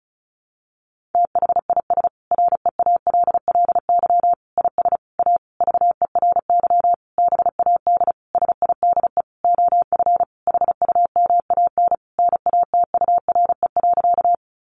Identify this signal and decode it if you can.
Morse code, heard as a keyed tone, about 34 wpm, 700 Hz. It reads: THIS REALLY IS A VERY BAD SIDE OF HUMAN NATURE.
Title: Sentences from Top 700 Words 35wpm